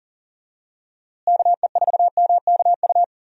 Morse code, heard as a keyed tone, about 37 wpm, 700 Hz. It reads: KE4MKU
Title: USA Call Signs - 6 Characters - Repeated 3x - 40wpm